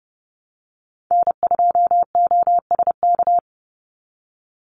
Morse code, heard as a keyed tone, about 30 wpm, 700 Hz. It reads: N2OSK
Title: USA Call Signs 30wpm